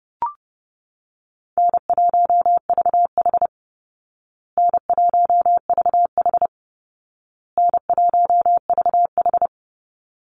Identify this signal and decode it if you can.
Morse code, heard as a keyed tone, about 30 wpm, 700 Hz. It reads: N1VH N1VH N1VH
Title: Call Signs Repeated 2x - 30wpm